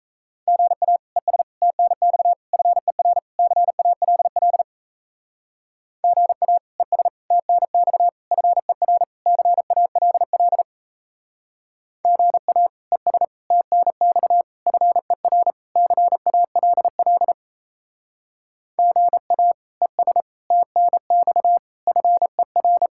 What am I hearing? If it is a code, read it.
Morse code, heard as a keyed tone, about 34 wpm, 700 Hz. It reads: GA ES TNX FER CALL GA ES TNX FER CALL GA ES TNX FER CALL GA ES TNX FER
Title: Speed-Racing Common QSO Phrases - 28wpm